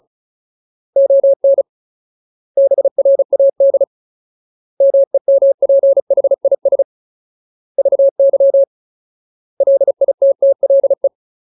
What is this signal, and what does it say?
Morse code, heard as a keyed tone, about 34 wpm, 550 Hz. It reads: ON BRAD MEMPHIS VY LITTLE
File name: Ultimate Word List / 35WPM Ultimate